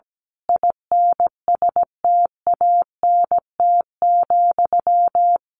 Morse code, heard as a keyed tone, about 17 wpm, 700 Hz. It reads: INSTANT,
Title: A Sound Of Thunder 17wpm